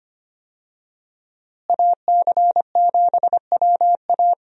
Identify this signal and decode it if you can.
Morse code, heard as a keyed tone, about 25 wpm, 700 Hz. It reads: AC7WA
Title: USA Call Signs - 5 Characters - Repeated 2x - 25wpm